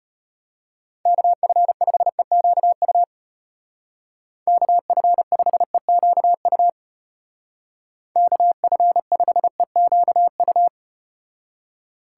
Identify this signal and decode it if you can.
Morse code, heard as a keyed tone, about 35 wpm, 700 Hz. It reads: KF5EQU KF5EQU KF5EQU